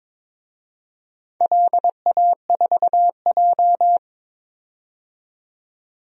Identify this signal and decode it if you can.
Morse code, heard as a keyed tone, about 22 wpm, 700 Hz. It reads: LA4J
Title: DXCC-100 Call Signs - 4 Characters - 22wpm